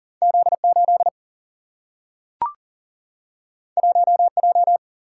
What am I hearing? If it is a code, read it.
Morse code, heard as a keyed tone, about 38 wpm, 700 Hz. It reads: Z8 1J
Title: Single-Letter Single-Number 40wpm